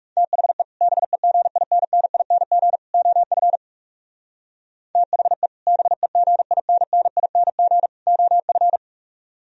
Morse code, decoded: THE BEGINNING OF THE BEGINNING OF

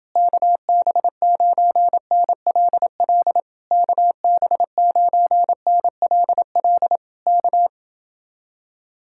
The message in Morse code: KB9NLL KB9NLL K

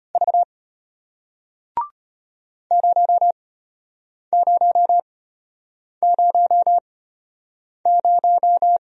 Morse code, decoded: V 0000

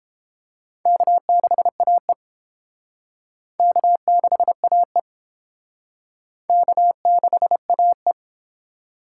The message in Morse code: K6AE K6AE K6AE

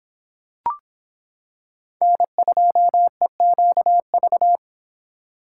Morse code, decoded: N2EQV

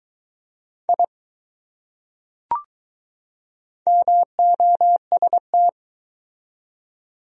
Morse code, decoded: I MOST

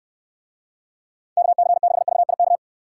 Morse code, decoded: KC6VIK

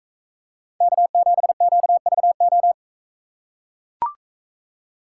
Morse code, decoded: K7QVO

